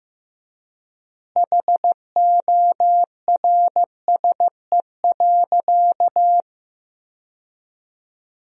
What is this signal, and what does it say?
Morse code, heard as a keyed tone, about 15 wpm, 700 Hz. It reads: HORSE.
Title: 4x Word-Spacing - Sentences from Top 300 Words - 15wpm